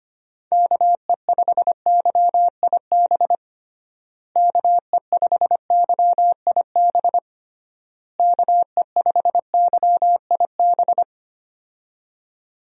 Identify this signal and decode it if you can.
Morse code, heard as a keyed tone, about 25 wpm, 700 Hz. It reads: KE5YIB KE5YIB KE5YIB
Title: USA Call Signs - 6 Characters - Repeated 3x - 25wpm